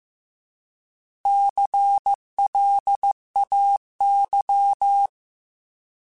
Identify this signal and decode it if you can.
Morse code, heard as a keyed tone, about 15 wpm, 800 Hz. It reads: CLAY